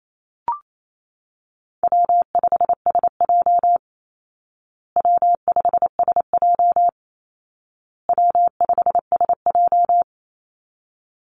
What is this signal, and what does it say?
Morse code, heard as a keyed tone, about 28 wpm, 700 Hz. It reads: W5SJ W5SJ W5SJ
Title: Call Signs Repeated 2x - 28wpm